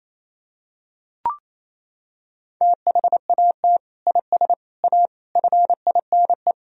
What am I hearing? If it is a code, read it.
Morse code, heard as a keyed tone, about 28 wpm, 700 Hz. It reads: THAT IS A FINE
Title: Sentences from Top 1000 Words - Review Part 4 - 28wpm